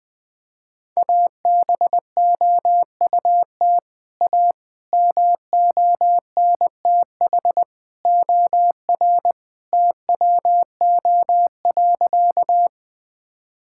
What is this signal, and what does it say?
Morse code, heard as a keyed tone, about 20 wpm, 700 Hz. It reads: ABOUT A MONTH OR TWO.
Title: Sentences from Top 1000 Words - Review Part 5 - 20wpm